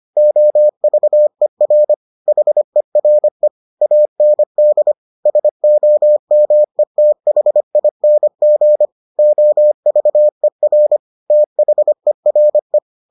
Morse code, heard as a keyed tone, about 25 wpm, 600 Hz. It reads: OVER HERE AND SOMETHING OVER THERE